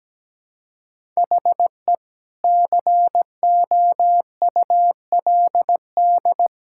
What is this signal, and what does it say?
Morse code, heard as a keyed tone, about 17 wpm, 700 Hz. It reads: HE COULD
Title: Sentences from Top 1000 Words - Review Part 6 - 17wpm